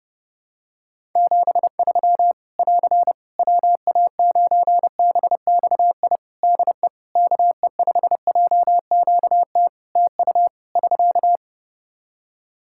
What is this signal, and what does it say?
Morse code, heard as a keyed tone, about 30 wpm, 700 Hz. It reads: 73 <AR> WA9BXI DE KE5JQT TU <SK>
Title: Practice QSO - Exchange 4 - Ending - 30wpm